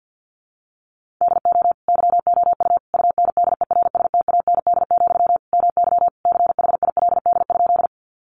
Morse code, heard as a keyed tone, about 50 wpm, 700 Hz. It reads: DO YOU UNDERSTAND, MY CHILD?
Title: Sentences from Top 800 Words 50wpm